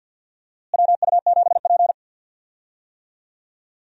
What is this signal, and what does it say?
Morse code, heard as a keyed tone, about 51 wpm, 700 Hz. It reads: WU7P
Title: USA Call Signs - 4 Characters - Repeated 2x - 50wpm